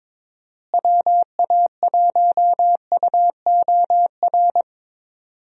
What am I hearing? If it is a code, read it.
Morse code, heard as a keyed tone, about 21 wpm, 700 Hz. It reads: WA1UOR
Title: USA Call Signs - 6 Characters - Repeated 3x - 22wpm